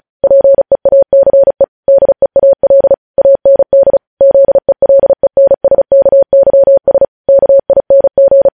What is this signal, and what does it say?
Morse code, heard as a keyed tone, about 36 wpm, 550 Hz. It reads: PEACE DEAL AND ZELENSKYS KING
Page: News Headlines in Morse Code at 35 WPM